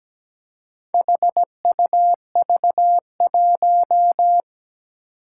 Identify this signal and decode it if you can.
Morse code, heard as a keyed tone, about 17 wpm, 700 Hz. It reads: HUV1
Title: Four Challenging Letter-Number Combinations 17wpm